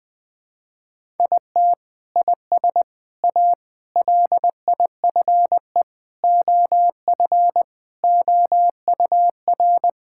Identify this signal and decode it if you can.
Morse code, heard as a keyed tone, about 21 wpm, 700 Hz. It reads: IT IS A LIFE OF OUR
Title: Sentences from Top 1000 Words - Review Part 5 - 20wpm